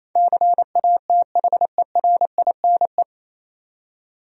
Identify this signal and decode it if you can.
Morse code, heard as a keyed tone, about 28 wpm, 700 Hz. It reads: CATHERINE
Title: USA Common Names - 28wpm